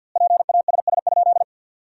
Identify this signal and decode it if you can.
Morse code, heard as a keyed tone, about 50 wpm, 700 Hz. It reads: PASS?
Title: 4x Word-Spacing - Sentences from Top 400 Words - 50wpm